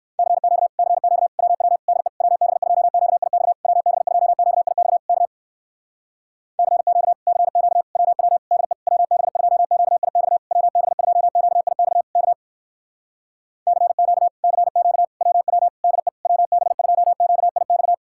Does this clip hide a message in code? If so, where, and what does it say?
Yes: Morse code from 0.2 to 17.9 s, heard as a keyed tone, about 66 wpm, 700 Hz. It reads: CQ CQ WW DE WB2QIX WB2QIX K CQ CQ WW DE WB2QIX WB2QIX K CQ CQ WW DE WB2QIX